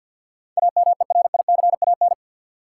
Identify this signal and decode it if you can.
Morse code, heard as a keyed tone, about 48 wpm, 700 Hz. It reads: AMERICAN
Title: Sets of 2 Words - New Word Drill - 50wpm